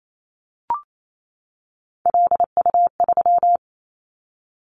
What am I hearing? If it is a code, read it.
Morse code, heard as a keyed tone, about 28 wpm, 700 Hz. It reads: LU3
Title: Callsign Prefixes for Top 50 DXCC - 28wpm